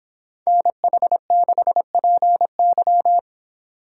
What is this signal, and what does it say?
Morse code, heard as a keyed tone, about 29 wpm, 700 Hz. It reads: NH6PY